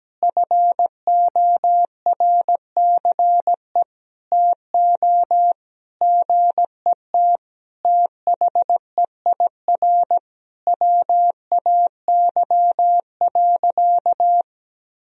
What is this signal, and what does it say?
Morse code, heard as a keyed tone, about 17 wpm, 700 Hz. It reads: FORCE TO GET THEIR WAY.
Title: Sentences from Top 1000 Words - Review Part 4 - 17wpm